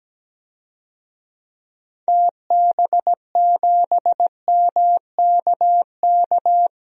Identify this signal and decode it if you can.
Morse code, heard as a keyed tone, about 17 wpm, 700 Hz. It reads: TB7MKK
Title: DXCC-100 Call Signs - 6 Characters - Repeated 2x - 17wpm